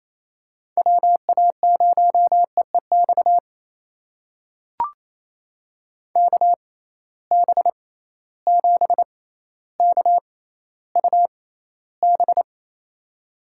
Morse code, heard as a keyed tone, about 28 wpm, spaced as for 14 wpm, 700 Hz. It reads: WA0EEX KB7KUB
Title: USA Call Signs - 6 Characters - 6x Character Spacing - 28wpm